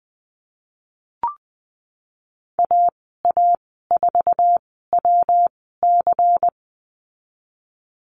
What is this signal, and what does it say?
Morse code, heard as a keyed tone, about 20 wpm, spaced as for 15 wpm, 700 Hz. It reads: AA4WC
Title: 219 Lesson 35g - Number 0 - Call Signs 20wpm